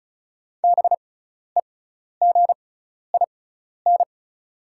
Morse code, heard as a keyed tone, about 36 wpm, spaced as for 12 wpm, 700 Hz. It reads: BEGIN
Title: 6x Character Spacing - Top 300 Words - 35wpm